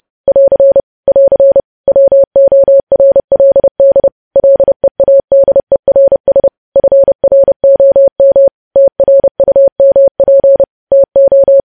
Morse code, heard as a keyed tone, about 30 wpm, 550 Hz. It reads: <AR> <AR> WORLD LEADERS FROM TRUMP TO
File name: podcast / News headlines-1x-30x30